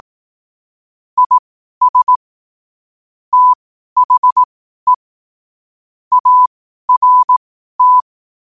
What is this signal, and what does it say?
Morse code, heard as a keyed tone, about 18 wpm, spaced as for 12 wpm, 1000 Hz. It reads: IS THE ART
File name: qotdfiles / 12wpm